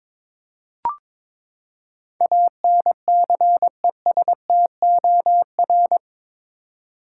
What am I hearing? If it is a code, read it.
Morse code, heard as a keyed tone, about 22 wpm, 700 Hz. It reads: ANCESTOR
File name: practice / 22wpm / Eight Letter Words - 22wpm